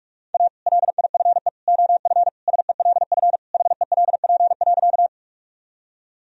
Morse code, heard as a keyed tone, about 41 wpm, 700 Hz. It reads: A LIFE OF SELF HELP.
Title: Sentences from Top 1000 Words - Review Part 1 - 45wpm